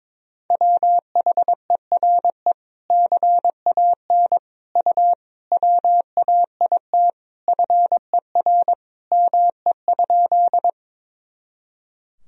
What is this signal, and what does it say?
Morse code, heard as a keyed tone, about 22 wpm, 700 Hz. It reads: WHERE CAN U WAIT FER ME?
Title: Sentences from Top 500 Words and Abbreviations 22wpm